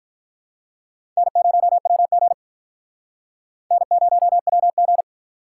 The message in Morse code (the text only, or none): N0WG N0WG